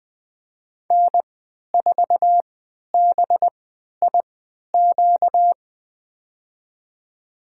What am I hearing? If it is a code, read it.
Morse code, heard as a keyed tone, about 20 wpm, spaced as for 11 wpm, 700 Hz. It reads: N4BIQ
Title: USA Call Signs - 5 Characters - 3x Character Spacing - 20wpm